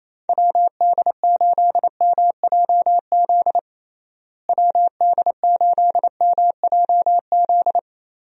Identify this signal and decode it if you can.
Morse code, heard as a keyed tone, about 28 wpm, 700 Hz. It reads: WD8MJZ WD8MJZ
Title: USA Call Signs - 6 Characters - Repeated 3x - 28wpm